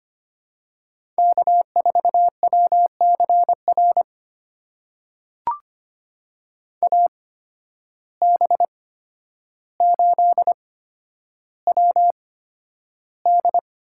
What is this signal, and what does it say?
Morse code, heard as a keyed tone, about 25 wpm, spaced as for 11 wpm, 700 Hz. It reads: K4WCR AB8WD
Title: USA Call Signs - 8x Character Spacing - 25wpm